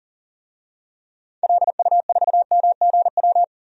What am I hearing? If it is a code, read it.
Morse code, heard as a keyed tone, about 40 wpm, 700 Hz. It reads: LU4MGW
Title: DXCC-100 Call Signs - 6 Characters - 40wpm